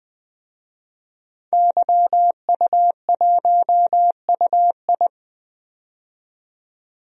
Morse code, heard as a keyed tone, about 20 wpm, 700 Hz. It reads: YU1UI
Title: DXCC-100 Call Signs - 5 Characters - 20wpm